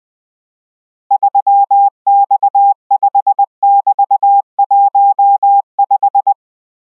Morse code, heard as a keyed tone, about 20 wpm, 800 Hz. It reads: 3X5<BT>15